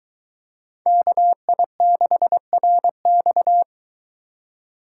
Morse code, heard as a keyed tone, about 23 wpm, 700 Hz. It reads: KI6RX